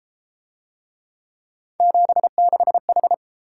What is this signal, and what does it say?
Morse code, heard as a keyed tone, about 29 wpm, 700 Hz. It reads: 76H